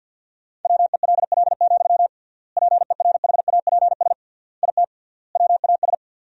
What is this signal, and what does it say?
Morse code, heard as a keyed tone, about 51 wpm, 700 Hz. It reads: WELL, PERHAPS IT WAS
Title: 3x Word-Spacing - Sentences from Top 500 Words - 50wpm